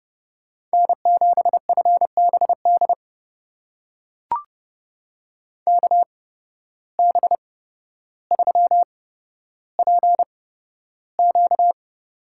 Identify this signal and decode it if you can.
Morse code, heard as a keyed tone, about 32 wpm, spaced as for 13 wpm, 700 Hz. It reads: N7FBD KB3PQ
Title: USA Call Signs - 5 Characters - 8x Character Spacing - 30wpm